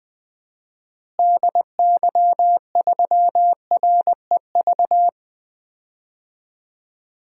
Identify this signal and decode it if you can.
Morse code, heard as a keyed tone, about 21 wpm, 700 Hz. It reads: DY3REV
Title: DXCC-100 Call Signs - 6 Characters - Repeated 3x - 20wpm